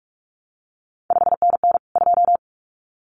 Morse code, heard as a keyed tone, about 46 wpm, 700 Hz. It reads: 5NN 2
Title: CQ WW DX Contest Exchange - 45wpm